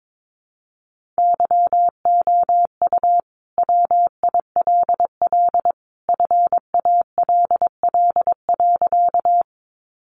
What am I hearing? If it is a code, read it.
Morse code, heard as a keyed tone, about 22 wpm, 700 Hz. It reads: YOU WILL FALL.
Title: Sentences from Top 500 Words 22wpm